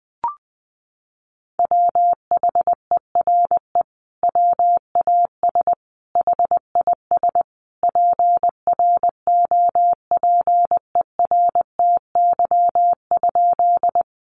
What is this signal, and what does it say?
Morse code, heard as a keyed tone, about 20 wpm, 700 Hz. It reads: WHERE WAS HIS PROPERTY?
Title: Sentences from Top 1000 Words 20wpm